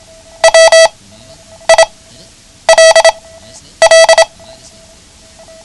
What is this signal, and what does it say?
Morse code, heard as a keyed tone, about 27 wpm, spaced as for 9 wpm, 650 Hz. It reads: WILL
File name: ham radio / CodeQuick / Code Quick 09